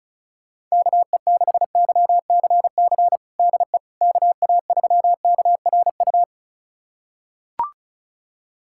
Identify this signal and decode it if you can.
Morse code, heard as a keyed tone, about 36 wpm, 700 Hz. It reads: KE6YCC DE KA3KRU
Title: USA Callsign DE Callsign 35wpm